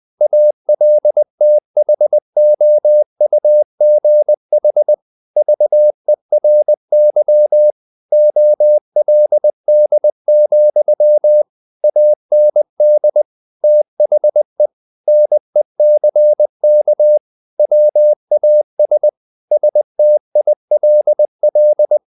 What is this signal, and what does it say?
Morse code, heard as a keyed tone, about 20 wpm, 600 Hz. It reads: ALTHOUGH VERY OLD, AND THE NECK WAS STILL
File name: the old man and the sea / the old man and the sea part 1 20wpm